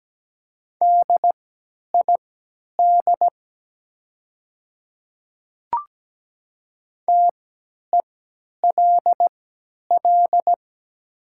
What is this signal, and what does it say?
Morse code, heard as a keyed tone, about 17 wpm, spaced as for 10 wpm, 700 Hz. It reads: DID TELL
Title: 3x Character Spacing - Top 200 Words - 17wpm